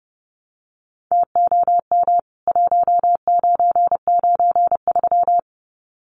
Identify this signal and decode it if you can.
Morse code, heard as a keyed tone, about 30 wpm, 700 Hz. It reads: TOM 1993
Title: CWT Exchange 30wpm